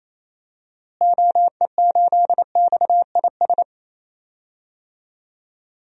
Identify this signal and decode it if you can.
Morse code, heard as a keyed tone, about 28 wpm, 700 Hz. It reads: OE8XIS
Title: DXCC-100 Call Signs - 6 Characters - Repeated 3x - 28wpm